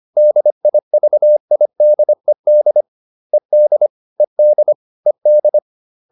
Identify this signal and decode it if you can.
Morse code, heard as a keyed tone, about 25 wpm, 600 Hz. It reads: DIVIDED ED ED ED